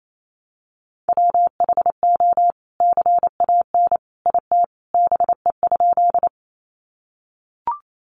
Morse code, heard as a keyed tone, about 28 wpm, 700 Hz. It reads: WHO CAN IT BE?
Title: Sentences from Top 100 Words 28wpm